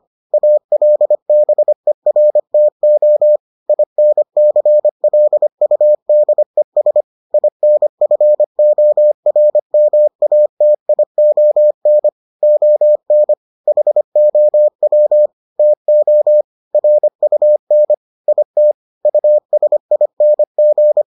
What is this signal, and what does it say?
Morse code, heard as a keyed tone, about 25 wpm, 600 Hz. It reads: ALBERTO INCLUDES INFORMATION ON HOW TO RUN IT USING